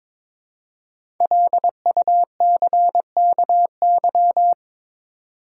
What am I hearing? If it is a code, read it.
Morse code, heard as a keyed tone, about 22 wpm, 700 Hz. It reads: LUCKY
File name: 5 Letter Words - First 3 Letters Given - 22wpm